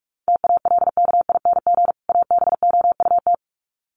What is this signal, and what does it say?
Morse code, heard as a keyed tone, about 44 wpm, 700 Hz. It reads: TALKING ABOUT
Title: Sets of 2 Words 45wpm